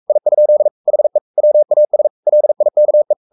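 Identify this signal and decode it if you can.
Morse code, heard as a keyed tone, about 44 wpm, 600 Hz. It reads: I? HE WAS LIKE
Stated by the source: Morse code at 43 wpm sending Ken Follett's novel Never in English, from CW med Gnister.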